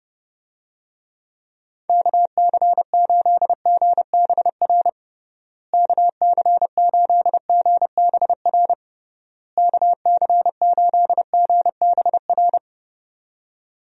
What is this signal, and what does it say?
Morse code, heard as a keyed tone, about 32 wpm, 700 Hz. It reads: KC8GBR KC8GBR KC8GBR
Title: USA Call Signs - 6 Characters - Repeated 3x - 30wpm